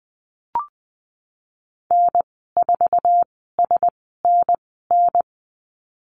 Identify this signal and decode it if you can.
Morse code, heard as a keyed tone, about 19 wpm, spaced as for 14 wpm, 700 Hz. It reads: N4SNN
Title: 076 Lesson 13g - Letter L - Call Signs 20wpm